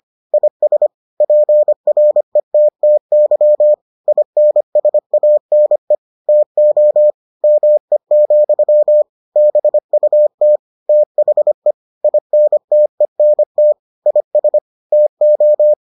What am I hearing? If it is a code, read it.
Morse code, heard as a keyed tone, about 25 wpm, 600 Hz. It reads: IS PRETTY INSANE TO ME, BUT THE INTENT IS TO